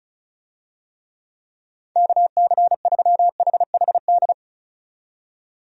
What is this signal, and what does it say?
Morse code, heard as a keyed tone, about 34 wpm, 700 Hz. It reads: KC3HHD
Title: USA Call Signs - 6 Characters - 35wpm